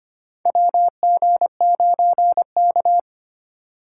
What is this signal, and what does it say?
Morse code, heard as a keyed tone, about 25 wpm, 700 Hz. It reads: WG9K
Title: USA Call Signs - 6x Character Spacing - 25wpm